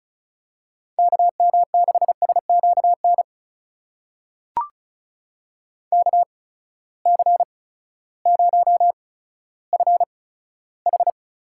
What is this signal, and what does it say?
Morse code, heard as a keyed tone, about 36 wpm, spaced as for 16 wpm, 700 Hz. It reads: KM6SQN KC0FH
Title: USA Call Signs - 6 Characters - 8x Character Spacing - 35wpm